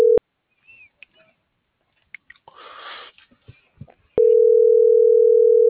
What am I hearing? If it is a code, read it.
Ringback tone (North America), 440 + 480 Hz.